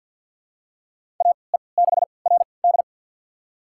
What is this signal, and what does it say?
Morse code, heard as a keyed tone, about 47 wpm, spaced as for 28 wpm, 700 Hz. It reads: AE6RD
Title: USA Call Signs - 5 Characters - 3x Character Spacing - 50wpm